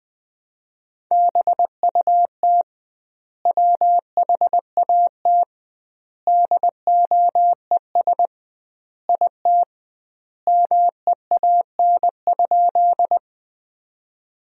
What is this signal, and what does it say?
Morse code, heard as a keyed tone, about 20 wpm, 700 Hz. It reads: BUT WHAT DOES IT MEAN?
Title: 2x Word-Spacing - Sentences from Top 200 Words - 20wpm